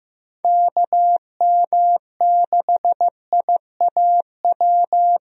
Morse code, heard as a keyed tone, about 15 wpm, 700 Hz. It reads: KM6IAW
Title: USA Call Signs - 6 Characters - Repeated 2x - 15wpm